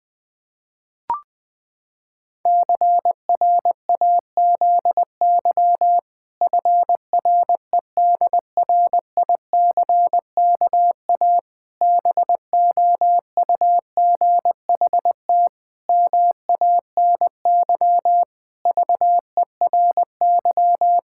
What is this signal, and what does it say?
Morse code, heard as a keyed tone, about 20 wpm, 700 Hz. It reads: CRAZY FREDRICKA BOUGHT MANY VERY
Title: Pangram Challenge v2 - 20wpm